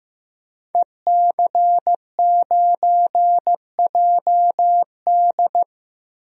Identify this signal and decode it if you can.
Morse code, heard as a keyed tone, about 15 wpm, 700 Hz. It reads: EC9JD
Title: DXCC-100 Call Signs - 5 Characters - 15wpm